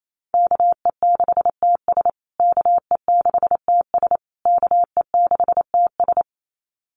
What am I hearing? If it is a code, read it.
Morse code, heard as a keyed tone, about 28 wpm, 700 Hz. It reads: KE6TS KE6TS KE6TS
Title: USA Call Signs Repeated 3x 28wpm